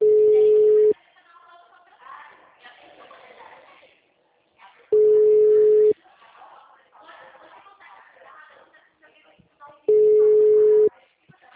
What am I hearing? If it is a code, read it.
Ringback tone (Europe, 425 Hz), with speech.